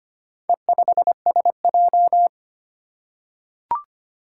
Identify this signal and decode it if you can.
Morse code, heard as a keyed tone, about 25 wpm, 700 Hz. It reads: E5SJ
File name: practice / 25wpm / Four Letter-Number Combinations - 25wpm